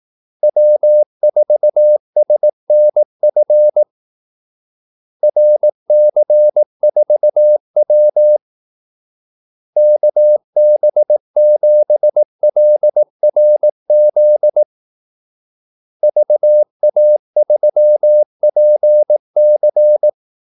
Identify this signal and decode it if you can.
Morse code, heard as a keyed tone, about 18 wpm, 600 Hz. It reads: W4SNF RC4W KB7LRZ VA3PC
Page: Kaldesignaler 18 wpm